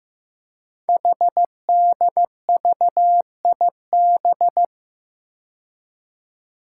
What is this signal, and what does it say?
Morse code, heard as a keyed tone, about 15 wpm, 700 Hz. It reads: HDVIB
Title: Code Groups - 5 Letters - 15wpm